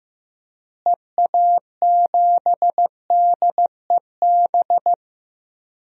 Morse code, heard as a keyed tone, about 15 wpm, 700 Hz. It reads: EA7DEB
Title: DXCC-100 Call Signs - 6 Characters - Repeated 2x - 15wpm